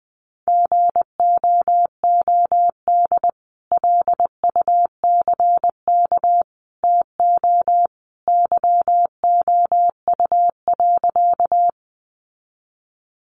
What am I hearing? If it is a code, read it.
Morse code, heard as a keyed tone, about 20 wpm, 700 Hz. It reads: GOOD LUCK TO YOU.
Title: Sentences from Top 1500 Words 20wpm